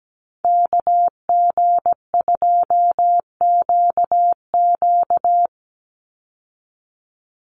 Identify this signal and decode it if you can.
Morse code, heard as a keyed tone, about 17 wpm, 700 Hz. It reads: KG2QQ
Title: USA Call Signs 17wpm